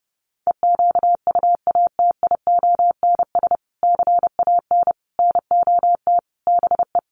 Morse code, heard as a keyed tone, about 28 wpm, 700 Hz. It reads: EQUATIONS CAN NOT BE
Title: Sets of 5 Words 30wpm